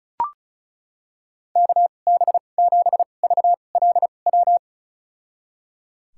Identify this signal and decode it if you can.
Morse code, heard as a keyed tone, about 34 wpm, spaced as for 25 wpm, 700 Hz. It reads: KB7VLW
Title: USA Call Signs - 6 Characters - 2x Character Spacing - 35wpm